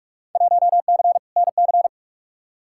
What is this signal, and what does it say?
Morse code, heard as a keyed tone, about 42 wpm, 700 Hz. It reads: 1C NC
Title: Field Day Exchanges - 45wpm